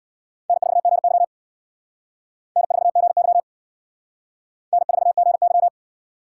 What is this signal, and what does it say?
Morse code, heard as a keyed tone, about 60 wpm, 700 Hz. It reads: N4GY N4GY N4GY
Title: Speed Racing - USA 1x2 Call Signs 50wpm